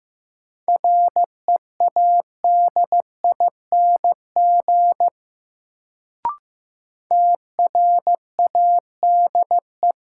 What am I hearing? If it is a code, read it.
Morse code, heard as a keyed tone, about 15 wpm, 700 Hz. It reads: READING TRADE
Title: Sentences from Top 600 Words - New Word Drill - 15wpm